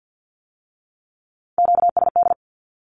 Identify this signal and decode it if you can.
Morse code, heard as a keyed tone, about 50 wpm, 700 Hz. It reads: QSD
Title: Q-Codes with Spoken Meaning - 50wpm